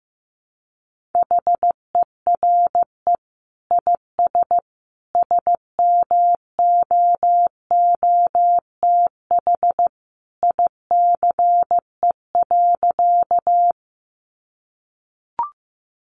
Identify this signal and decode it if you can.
Morse code, heard as a keyed tone, about 15 wpm, 700 Hz. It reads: HERE IS SMOOTH ICE.
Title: Sentences from Top 1400 Words 15wpm